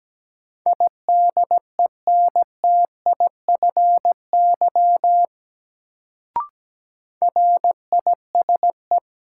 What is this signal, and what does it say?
Morse code, heard as a keyed tone, about 17 wpm, 700 Hz. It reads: IDENTIFY RISE
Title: Sentences from Top 900 Words - New Word Drill - 17wpm